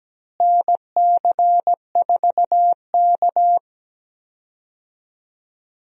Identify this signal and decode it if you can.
Morse code, heard as a keyed tone, about 17 wpm, 700 Hz. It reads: NC4K
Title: USA Call Signs - 4 Characters - Repeated 2x - 17wpm